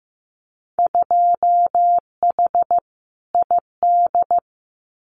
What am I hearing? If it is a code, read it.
Morse code, heard as a keyed tone, about 15 wpm, 700 Hz. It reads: 2H ID
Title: Winter Field Day Exchanges 15wpm